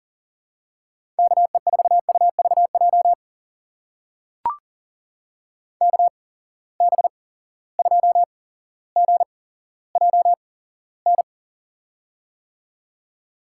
Morse code, heard as a keyed tone, about 39 wpm, spaced as for 17 wpm, 700 Hz. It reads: KE4UVJ KB2GJN
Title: USA Call Signs - 6 Characters - 8x Character Spacing - 40wpm